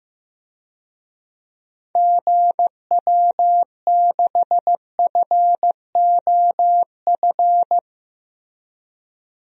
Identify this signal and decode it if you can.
Morse code, heard as a keyed tone, about 16 wpm, 700 Hz. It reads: GW6FOF
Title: DXCC-100 Call Signs - 6 Characters - Repeated 2x - 15wpm